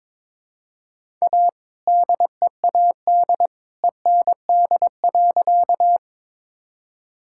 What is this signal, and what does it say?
Morse code, heard as a keyed tone, about 22 wpm, 700 Hz. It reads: A DEAD END.
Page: morse-code-ninja-practice-22wpm